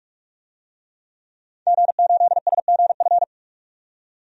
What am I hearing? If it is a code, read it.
Morse code, heard as a keyed tone, about 47 wpm, 700 Hz. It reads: G8SGF
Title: DXCC-100 Call Signs - 5 Characters - 45wpm